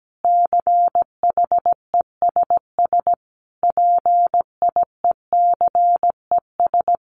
Morse code, heard as a keyed tone, about 17 wpm, 700 Hz. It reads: CHESS PIECES
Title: A Sound Of Thunder 17wpm